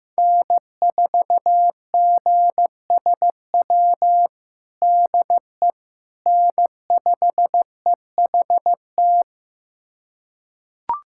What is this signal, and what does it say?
Morse code, heard as a keyed tone, about 15 wpm, 700 Hz. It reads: N4GSW DE N5EHT
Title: USA Callsign DE Callsign - 5 Characters - 15wpm